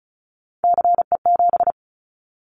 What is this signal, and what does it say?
Morse code, heard as a keyed tone, about 34 wpm, 700 Hz. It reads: CE7
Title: Callsign Prefixes for Top 100 DXCC - 35wpm